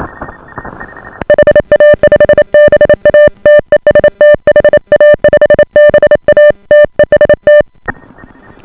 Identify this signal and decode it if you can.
Morse code, heard as a keyed tone, about 28 wpm, 590 Hz. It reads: HA5BA TEST HA5BA TEST